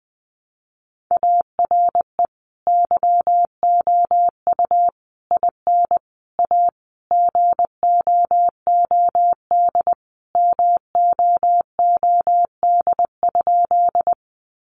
Morse code, heard as a keyed tone, about 19 wpm, 700 Hz. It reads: ARE YOU IN A GOOD MOOD?
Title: Sentences from Top 1100 Words 20wpm